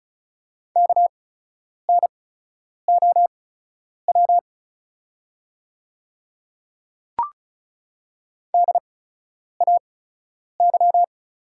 Morse code, heard as a keyed tone, about 36 wpm, spaced as for 10 wpm, 700 Hz. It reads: KNOW DAY
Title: 8x Character Spacing - Top 100 Words - 35wpm